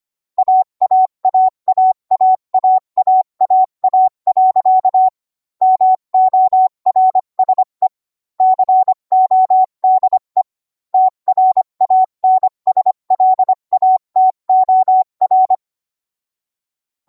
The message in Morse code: AAAAAAAAA. MORSE CODE TRANSLATOR